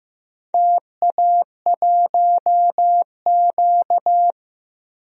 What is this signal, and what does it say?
Morse code, heard as a keyed tone, about 15 wpm, 700 Hz. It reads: TA1Q
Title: DXCC-100 Call Signs - 4 Characters - Repeated 2x - 15wpm